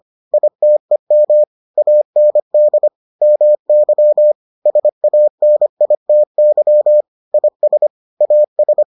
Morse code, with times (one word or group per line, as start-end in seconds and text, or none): 0.3-1.4 ITEM
1.8-2.9 AND
3.2-4.3 MY
4.7-7.0 SANITY
7.3-7.9 IS
8.2-8.8 AS